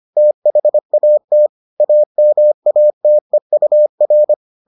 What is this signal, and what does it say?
Morse code, heard as a keyed tone, about 25 wpm, 600 Hz. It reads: THAT AMATEUR